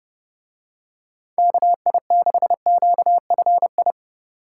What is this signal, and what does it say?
Morse code, heard as a keyed tone, about 32 wpm, 700 Hz. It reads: KI6QFI